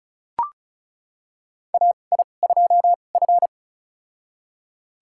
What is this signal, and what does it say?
Morse code, heard as a keyed tone, about 36 wpm, spaced as for 25 wpm, 700 Hz. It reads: AI2F
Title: USA Call Signs - 4 Characters - 2x Character Spacing - 35wpm